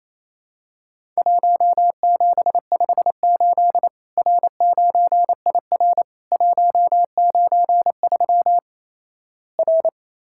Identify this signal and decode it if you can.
Morse code, heard as a keyed tone, about 28 wpm, 700 Hz. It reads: 1758 R9IR 193 R
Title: Worked All Europe DX Contest - QTC Exchange - 28wpm